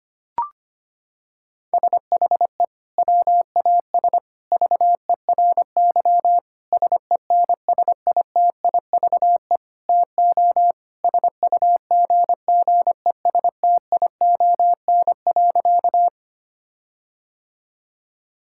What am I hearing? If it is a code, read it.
Morse code, heard as a keyed tone, about 25 wpm, 700 Hz. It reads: SHE WAS VERY SENSITIVE TO SUGGESTION.
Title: Sentences from Top 1600 Words 25wpm